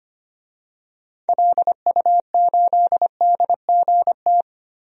Morse code, heard as a keyed tone, about 25 wpm, 700 Hz. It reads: LU8DGT
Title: DXCC-100 Call Signs - 6 Characters - 25wpm